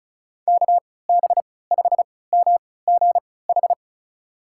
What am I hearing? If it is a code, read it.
Morse code, heard as a keyed tone, about 36 wpm, spaced as for 20 wpm, 700 Hz. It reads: KB5MGH
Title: USA Call Signs - 6 Characters - 3x Character Spacing - 35wpm